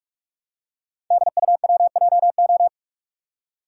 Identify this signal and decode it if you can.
Morse code, heard as a keyed tone, about 44 wpm, 690 Hz. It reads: DUWJO
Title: Code Groups - 5 Letter:Numbers - 45wpm